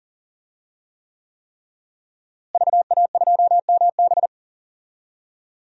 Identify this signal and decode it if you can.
Morse code, heard as a keyed tone, about 39 wpm, 700 Hz. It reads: VA2MB
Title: DXCC-100 Call Signs - 5 Characters - Repeated 3x - 40wpm